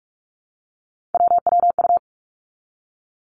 Morse code, heard as a keyed tone, about 46 wpm, 700 Hz. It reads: WWU
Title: Three Challenging Letter-Number Combinations 45wpm